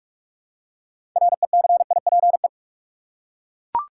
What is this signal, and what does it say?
Morse code, heard as a keyed tone, about 43 wpm, 700 Hz. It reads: RECIPE
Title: Six Letter Words - 45wpm